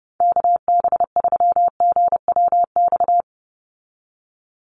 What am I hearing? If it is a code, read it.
Morse code, heard as a keyed tone, about 30 wpm, 700 Hz. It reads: KB3GWX
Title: USA Call Signs 30wpm